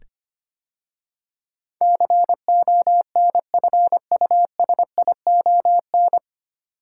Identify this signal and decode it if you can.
Morse code, heard as a keyed tone, about 25 wpm, 700 Hz. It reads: CONFUSION
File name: practice / 25wpm / 9 Letter Words - First 3 Letters Given - 25wpm